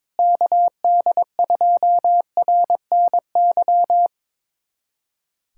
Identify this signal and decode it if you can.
Morse code, heard as a keyed tone, about 21 wpm, 700 Hz. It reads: KD2RNY
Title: USA Callsign DE Callsign 22wpm